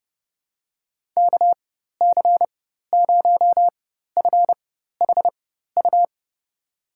Morse code, heard as a keyed tone, about 30 wpm, spaced as for 14 wpm, 700 Hz. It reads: KC0FHU
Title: USA Call Signs - 6 Characters - 4x Character Spacing - 30wpm